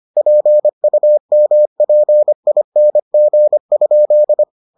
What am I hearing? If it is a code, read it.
Morse code, heard as a keyed tone, about 25 wpm, 600 Hz. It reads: PUMPING?